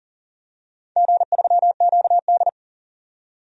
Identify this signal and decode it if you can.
Morse code, heard as a keyed tone, about 38 wpm, 700 Hz. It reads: G3QD
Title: DXCC-100 Call Signs - 4 Characters - 40wpm